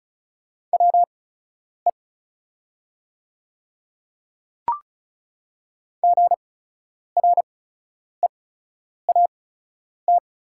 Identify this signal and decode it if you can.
Morse code, heard as a keyed tone, about 36 wpm, spaced as for 10 wpm, 700 Hz. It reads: WE GREAT
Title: 8x Character Spacing - Top 200 Words - 35wpm